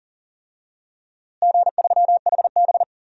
Morse code, heard as a keyed tone, about 40 wpm, 700 Hz. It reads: G3HB
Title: DXCC-100 Call Signs - 4 Characters - Repeated 3x - 40wpm